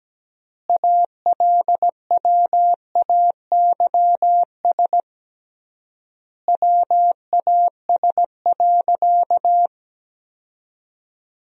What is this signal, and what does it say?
Morse code, heard as a keyed tone, about 17 wpm, 700 Hz. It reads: ALWAYS WAS.
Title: 3x Word-Spacing - Sentences from Top 300 Words - 17wpm